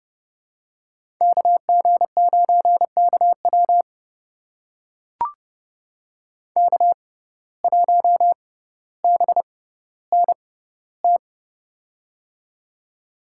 Morse code, heard as a keyed tone, about 31 wpm, spaced as for 15 wpm, 700 Hz. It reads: KG9KW K1BNT
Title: USA Call Signs - 5 Characters - 6x Character Spacing - 30wpm